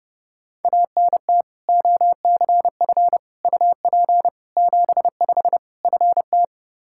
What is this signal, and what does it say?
Morse code, heard as a keyed tone, about 32 wpm, 700 Hz. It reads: ANT OCF UP 75 FT
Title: Practice QSO - Exchange 2 - Info - 30wpm